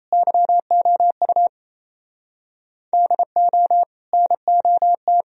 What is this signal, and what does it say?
Morse code, heard as a keyed tone, about 34 wpm, 700 Hz. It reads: YOU DO NOT
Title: Speed-Racing - Sets of 4 Words 28wpm